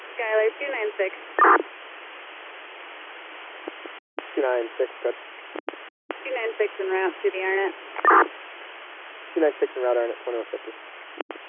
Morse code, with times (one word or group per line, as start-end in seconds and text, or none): none